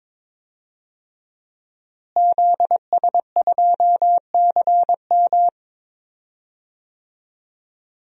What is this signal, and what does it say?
Morse code, heard as a keyed tone, about 21 wpm, 700 Hz. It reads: ZS2CM
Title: DXCC-100 Call Signs - 5 Characters - Repeated 2x - 22wpm